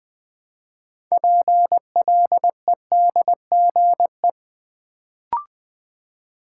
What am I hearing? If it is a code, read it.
Morse code, heard as a keyed tone, about 20 wpm, 700 Hz. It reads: PLEDGE
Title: Build A Word - 3 Letters At A Time - 6-Letter Words - 20wpm